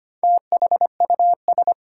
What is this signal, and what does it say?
Morse code, heard as a keyed tone, about 25 wpm, 700 Hz. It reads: THUS
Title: Sentences from Top 800 Words - New Word Drill - 25wpm